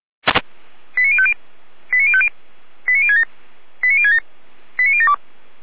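Five-tone selective call (ZVEI), heard as quick runs of five tones, tones of about 70 ms. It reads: 89050 89050 89067 89067 89072
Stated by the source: ZVEI five-tone selective call received by SDR.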